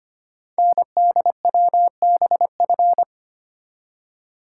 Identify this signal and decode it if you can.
Morse code, heard as a keyed tone, about 25 wpm, 700 Hz. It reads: NDWBF
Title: Code Groups - 5 Letters - 25wpm